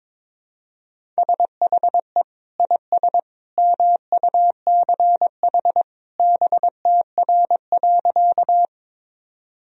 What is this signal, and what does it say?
Morse code, heard as a keyed tone, about 22 wpm, 700 Hz. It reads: SHE IS MUCH BTR.
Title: Sentences from Top 400 Words and Abbreviations 22wpm